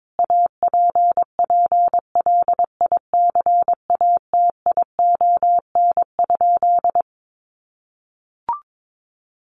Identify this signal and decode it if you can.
Morse code, heard as a keyed tone, about 22 wpm, 700 Hz. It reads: APPLICATION?
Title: Sentences from Top 1200 Words 22wpm